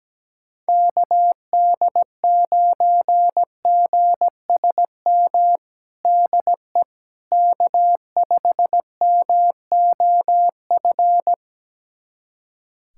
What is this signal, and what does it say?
Morse code, heard as a keyed tone, about 17 wpm, 700 Hz. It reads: KD9GSM DE K5MOF
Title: USA Callsign DE Callsign 17wpm